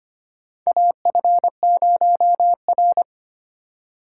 Morse code, heard as a keyed tone, about 25 wpm, 700 Hz. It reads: AF0R
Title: USA Call Signs - 3x Character Spacing - 25wpm